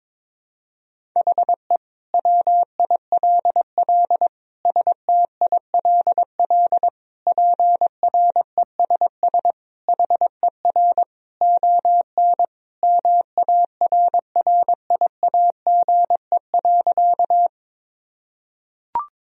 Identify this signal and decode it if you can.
Morse code, heard as a keyed tone, about 22 wpm, 700 Hz. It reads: HE WILL STILL PRESS HER ON MARRIAGE.
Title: Sentences from Top 1000 Words - Review Part 1 - 22wpm